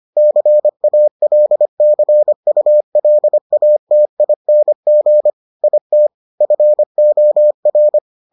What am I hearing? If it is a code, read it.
Morse code, heard as a keyed tone, about 25 wpm, 600 Hz. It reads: CALCULATING IT FOR